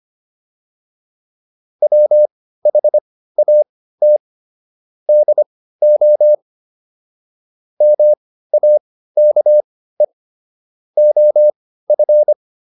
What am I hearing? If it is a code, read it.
Morse code, heard as a keyed tone, about 25 wpm, spaced as for 15 wpm, 600 Hz. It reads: WHAT DO MAKE OF